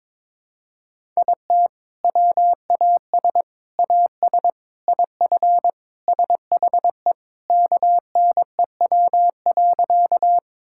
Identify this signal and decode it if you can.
Morse code, heard as a keyed tone, about 22 wpm, 700 Hz. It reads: IT WAS AS IF SHE KNEW.
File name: practice / 22wpm / Sentences with 4-Word Phrases - 22wpm